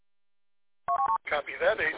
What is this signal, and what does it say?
Speech. There is no code.